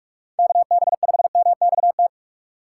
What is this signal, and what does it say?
Morse code, heard as a keyed tone, about 46 wpm, 700 Hz. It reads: KB5MXT